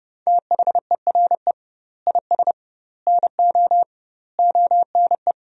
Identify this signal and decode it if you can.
Morse code, heard as a keyed tone, about 30 wpm, 700 Hz. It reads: THERE IS NO ONE
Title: 2x Word-Spacing - Sentences from Top 500 Words - 30wpm